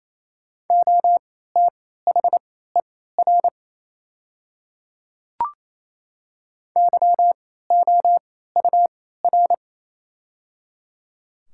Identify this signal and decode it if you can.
Morse code, heard as a keyed tone, about 28 wpm, spaced as for 16 wpm, 700 Hz. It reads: OTHER YOUR
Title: 3x Character Spacing - Top 200 Words - 28wpm